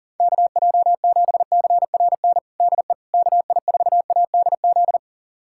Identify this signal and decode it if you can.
Morse code, heard as a keyed tone, about 40 wpm, 700 Hz. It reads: KJ7CRN DE KI4ADZ
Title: USA Callsign DE Callsign - 6 Characters - 40wpm